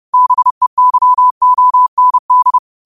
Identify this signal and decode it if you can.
Morse code, heard as a keyed tone, about 30 wpm, 1000 Hz. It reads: BEYOND